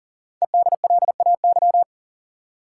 Morse code, heard as a keyed tone, about 39 wpm, 700 Hz. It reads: EDLAY
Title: Code Groups - 5 Letters - 40wpm